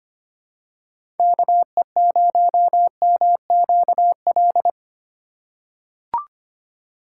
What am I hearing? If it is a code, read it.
Morse code, heard as a keyed tone, about 25 wpm, 700 Hz. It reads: KE0MQL
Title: USA Call Signs - 6 Characters - 8x Character Spacing - 25wpm